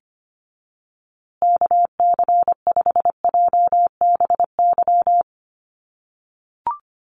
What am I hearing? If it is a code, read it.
Morse code, heard as a keyed tone, about 25 wpm, 700 Hz. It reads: KC5JBY
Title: USA Call Signs Repeated 3x 25wpm